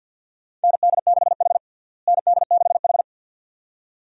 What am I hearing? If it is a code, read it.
Morse code, heard as a keyed tone, about 52 wpm, 700 Hz. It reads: ND6H ND6H
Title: USA Call Signs - 4 Characters - Repeated 2x - 50wpm